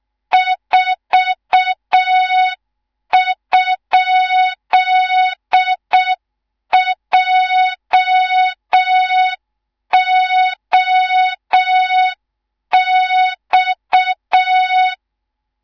Morse code, heard as a keyed tone, about 6 wpm, 3700 Hz. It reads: 4?JOX
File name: Morze-HA2MN / hang / 18 IRASJELEK2